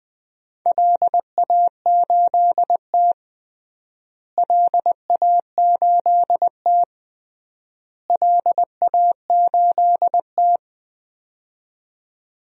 Morse code, heard as a keyed tone, about 20 wpm, 700 Hz. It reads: LA8T LA8T LA8T
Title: DXCC-100 Call Signs - 4 Characters - Repeated 3x - 20wpm